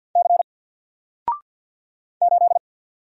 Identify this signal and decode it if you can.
Morse code, heard as a keyed tone, about 48 wpm, 700 Hz. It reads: C 8
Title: Single Letter-Number 50wpm